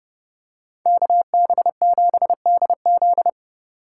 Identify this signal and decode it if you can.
Morse code, heard as a keyed tone, about 28 wpm, 700 Hz. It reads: KB7DZ